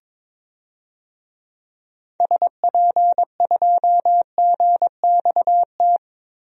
Morse code, heard as a keyed tone, about 22 wpm, 700 Hz. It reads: SP2GXT
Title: DXCC-100 Call Signs - 6 Characters - Repeated 2x - 22wpm